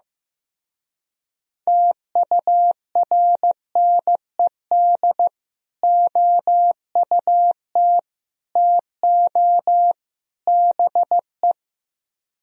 Morse code, heard as a keyed tone, about 15 wpm, 700 Hz. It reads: TURNED OUT TO BE
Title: Speed-Racing - Sets of 4 Words 15wpm